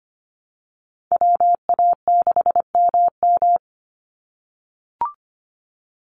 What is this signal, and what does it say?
Morse code, heard as a keyed tone, about 25 wpm, 700 Hz. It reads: WA6MM
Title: NA SOTA Chaser Callsigns 25wpm